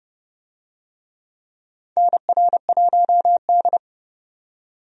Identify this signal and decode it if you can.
Morse code, heard as a keyed tone, about 30 wpm, 700 Hz. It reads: NR1D
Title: USA Call Signs - 4 Characters - 30wpm